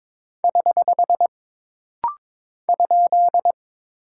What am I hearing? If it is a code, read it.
Morse code, heard as a keyed tone, about 22 wpm, 700 Hz. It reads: <HH> ?
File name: practice / 22wpm / Single Punctuation-Prosign - Rapid-Fire - 22wpm